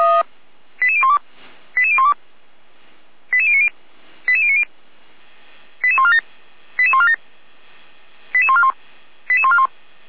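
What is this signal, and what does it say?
Five-tone selective call (ZVEI), heard as quick runs of five tones, tones of about 70 ms. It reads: 88012 88012 88090 88090 88147 88147 88141 88141
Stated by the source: ZVEI five-tone selective call received by SDR, from the Feuerwehr Orlamünde fire brigade.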